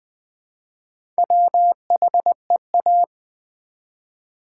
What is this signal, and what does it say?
Morse code, heard as a keyed tone, about 20 wpm, 700 Hz. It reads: WHEA
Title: Build A Word - 1 Letter At A Time - 5-Letter Words - 20wpm